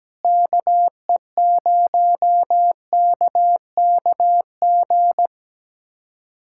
Morse code, heard as a keyed tone, about 17 wpm, 700 Hz. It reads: KE0KKG
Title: USA Callsign DE Callsign - 6 Characters - 17wpm